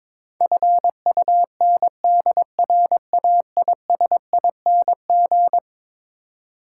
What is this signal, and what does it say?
Morse code, heard as a keyed tone, about 21 wpm, 700 Hz. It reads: FUNDRAISING
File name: practice / 22wpm / Eleven Letter Words - 22wpm